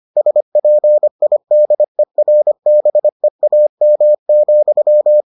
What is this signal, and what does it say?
Morse code, heard as a keyed tone, about 25 wpm, 600 Hz. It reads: SPIDERBEAM,